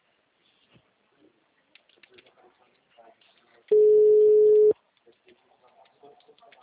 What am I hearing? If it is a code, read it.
Ringback tone (Europe, 425 Hz), with speech.